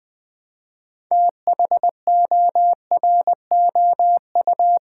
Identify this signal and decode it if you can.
Morse code, heard as a keyed tone, about 20 wpm, 700 Hz. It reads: THOROU
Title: Build A Word - 2 Letters At A Time - 8-Letter Words - 20wpm